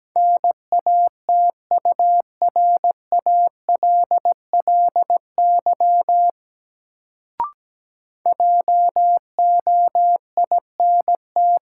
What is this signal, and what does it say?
Morse code, heard as a keyed tone, about 17 wpm, 700 Hz. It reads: NATURALLY JOINT
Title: Sentences from Top 1300 Words - New Word Drill - 17wpm